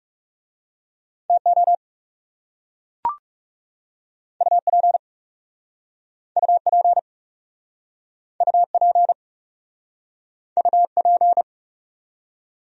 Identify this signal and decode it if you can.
Morse code, heard as a keyed tone, about 40 wpm, 700 Hz. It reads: TO UP UP UP UP